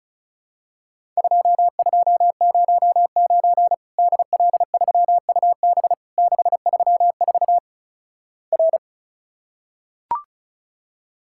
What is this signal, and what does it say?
Morse code, heard as a keyed tone, about 36 wpm, 700 Hz. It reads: 2209 DL3UB 634 R
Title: Worked All Europe DX Contest - QTC Exchange - 35wpm